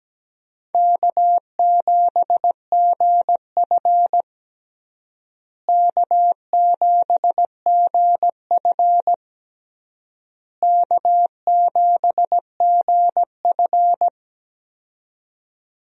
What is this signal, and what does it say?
Morse code, heard as a keyed tone, about 17 wpm, 700 Hz. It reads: K7GF K7GF K7GF
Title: USA Call Signs - 4 Characters - Repeated 3x - 17wpm